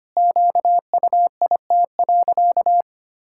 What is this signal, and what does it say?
Morse code, heard as a keyed tone, about 25 wpm, 700 Hz. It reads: QUIT.
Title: Pangram Challenge v2 - 25wpm